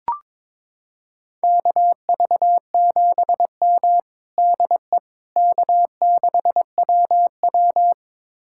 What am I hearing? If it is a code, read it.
Morse code, heard as a keyed tone, about 22 wpm, 700 Hz. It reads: KV7M DE K6WW
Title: USA Callsign DE Callsign - 4 Characters - 22wpm